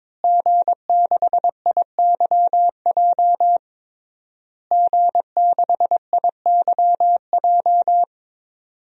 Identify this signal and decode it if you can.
Morse code, heard as a keyed tone, about 22 wpm, 700 Hz. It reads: G6IYJ G6IYJ
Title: DXCC-100 Call Signs - 5 Characters - Repeated 3x - 22wpm